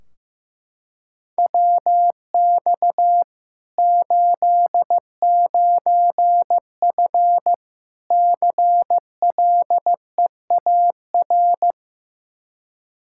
Morse code, heard as a keyed tone, about 15 wpm, 700 Hz. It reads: WX 89F CLEAR
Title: Practice QSO - Exchange 2 - Info - 15wpm